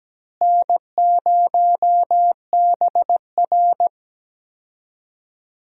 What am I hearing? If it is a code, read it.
Morse code, heard as a keyed tone, about 17 wpm, 700 Hz. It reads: N0BR
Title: USA Call Signs - 4 Characters - Repeated 3x - 17wpm